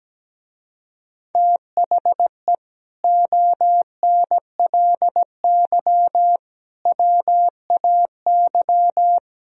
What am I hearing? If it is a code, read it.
Morse code, heard as a keyed tone, about 18 wpm, 700 Hz. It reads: THE ONLY WAY
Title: Speed-Racing - Sets of 4 Words 17wpm